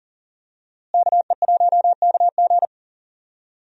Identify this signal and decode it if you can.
Morse code, heard as a keyed tone, about 39 wpm, 700 Hz. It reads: KE1KG
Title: USA Call Signs - 5 Characters - Repeated 2x - 40wpm